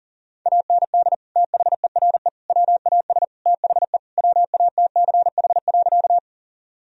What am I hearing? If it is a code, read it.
Morse code, heard as a keyed tone, about 40 wpm, 700 Hz. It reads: AND THERE WAS THE WATCH.
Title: Sentences from Top 1000 Words - Review Part 1 - 40wpm